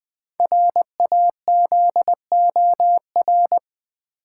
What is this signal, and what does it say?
Morse code, heard as a keyed tone, about 20 wpm, 700 Hz. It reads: RAZOR
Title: Easy Words with Hints - 20wpm